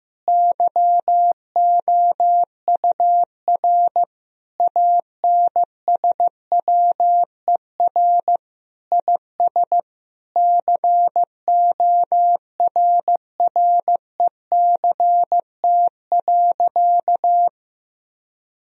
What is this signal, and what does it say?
Morse code, heard as a keyed tone, about 15 wpm, 700 Hz. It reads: YOUR ANSWER IS CORRECT.
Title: Sentences from Top 1000 Words - Review Part 2 - 15wpm